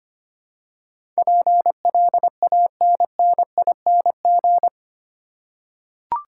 Morse code, heard as a keyed tone, about 25 wpm, 700 Hz. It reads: PLANNING
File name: 8 Letter Words - First 3 Letters Given - 25wpm